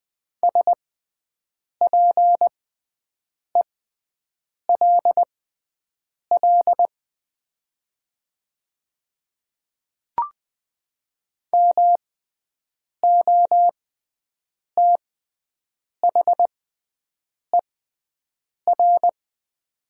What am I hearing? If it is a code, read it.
Morse code, heard as a keyed tone, about 20 wpm, spaced as for 7 wpm, 700 Hz. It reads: SPELL MOTHER
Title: 6x Character Spacing - Top 200 Words - 20wpm